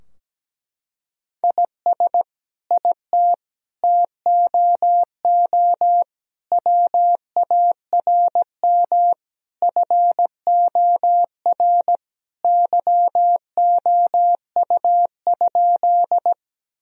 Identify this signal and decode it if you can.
Morse code, heard as a keyed tone, about 17 wpm, 700 Hz. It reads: IS IT TOO WARM FOR YOU?
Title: Sentences from Top 1000 Words - Review Part 2 - 17wpm